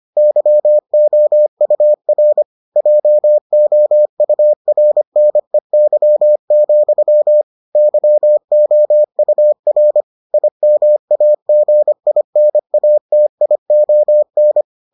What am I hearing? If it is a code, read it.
Morse code, heard as a keyed tone, about 25 wpm, 600 Hz. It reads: YOUR JOURNEY, YOUR IMAGINATION